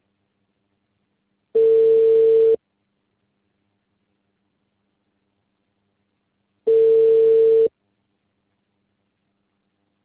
Ringback tone (China, 450 Hz).